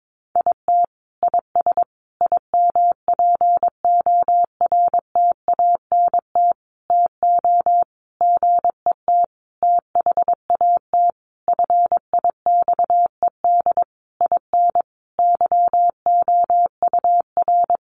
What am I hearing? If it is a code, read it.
Morse code, heard as a keyed tone, about 22 wpm, 700 Hz. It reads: IT IS IMPORTANT TO GET THAT FIXED IN YOUR
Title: Sentences from Top 1100 Words 22wpm